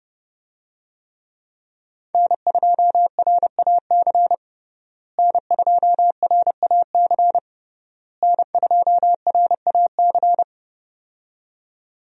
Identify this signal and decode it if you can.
Morse code, heard as a keyed tone, about 30 wpm, 700 Hz. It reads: N2RAC N2RAC N2RAC
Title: USA Call Signs - 5 Characters - Repeated 3x - 30wpm